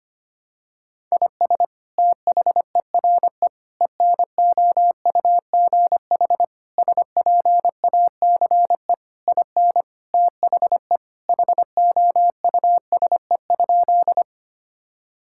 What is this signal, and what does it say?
Morse code, heard as a keyed tone, about 25 wpm, 700 Hz. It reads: IS THERE ENOUGH SPACE IN THE HOUSE?
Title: Sentences from Top 1000 Words - Review Part 5 - 25wpm